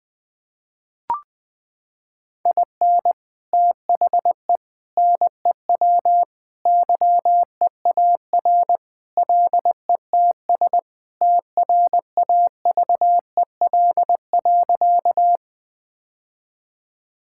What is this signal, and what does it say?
Morse code, heard as a keyed tone, about 21 wpm, 700 Hz. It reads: IN THE NEW YEAR LETS TRAVEL.
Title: Sentences with 3-Word Phrases - 20wpm